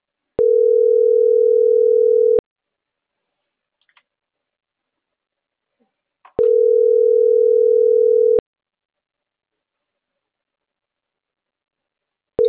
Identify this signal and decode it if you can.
Ringback tone (North America), 440 + 480 Hz.
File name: CENTRAL NMP / out marisol.monzalvonmp